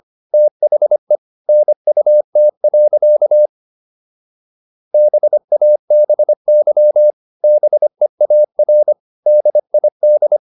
Morse code, heard as a keyed tone, about 25 wpm, 600 Hz. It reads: THE NUT. BABY BEAR DID